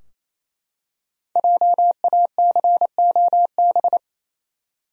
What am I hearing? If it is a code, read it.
Morse code, heard as a keyed tone, about 27 wpm, 700 Hz. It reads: JACOB
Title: USA Common Names - 28wpm